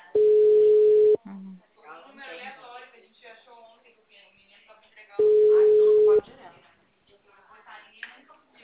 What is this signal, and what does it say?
Ringback tone (Europe, 425 Hz), with speech.
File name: out- -2012- - - .55326